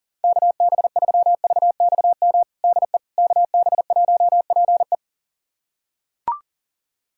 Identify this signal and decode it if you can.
Morse code, heard as a keyed tone, about 40 wpm, 700 Hz. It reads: KB3VXM DE KB1PE
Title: USA Callsign DE Callsign 40wpm